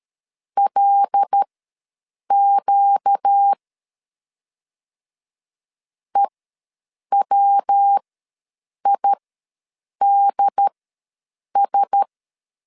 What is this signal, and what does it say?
Morse code, heard as a keyed tone, about 13 wpm, spaced as for 7 wpm, 800 Hz. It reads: LQ EWIDS
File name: LEZIONI / 5 LBVC / 5 L ws 12wpm 2958